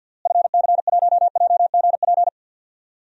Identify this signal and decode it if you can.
Morse code, heard as a keyed tone, about 48 wpm, 700 Hz. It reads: FC1JGP
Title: DXCC-100 Call Signs - 6 Characters - Repeated 3x - 50wpm